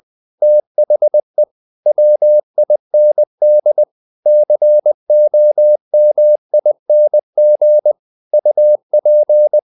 Morse code, heard as a keyed tone, about 20 wpm, 600 Hz. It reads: THE WIND COMING UP
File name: the old man and the sea / the old man and the sea part 1 20wpm